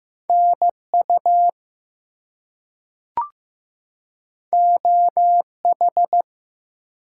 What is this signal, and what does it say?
Morse code, heard as a keyed tone, about 15 wpm, 700 Hz. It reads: NU OH
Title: US States and Canadian Provinces 15wpm